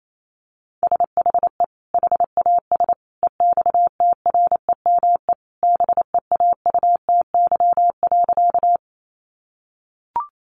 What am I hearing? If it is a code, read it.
Morse code, heard as a keyed tone, about 28 wpm, 700 Hz. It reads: SHE HAS EXTREME BEAUTY.
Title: Sentences from Top 1300 Words 28wpm